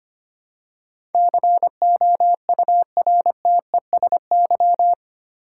Morse code, heard as a keyed tone, about 25 wpm, 700 Hz. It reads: COURTESY
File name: Eight Letter Words - 25wpm